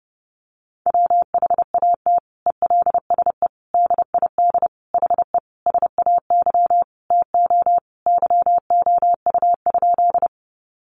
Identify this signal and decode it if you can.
Morse code, heard as a keyed tone, about 30 wpm, 700 Hz. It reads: WHAT ELSE DID HE SAY TO YOU?
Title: Sentences from Top 800 Words 30wpm